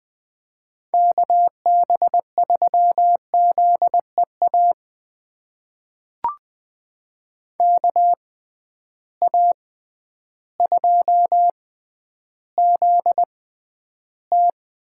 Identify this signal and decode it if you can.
Morse code, heard as a keyed tone, about 20 wpm, spaced as for 11 wpm, 700 Hz. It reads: KB3ZEA KA2ZT
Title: USA Call Signs - 6 Characters - 6x Character Spacing - 20wpm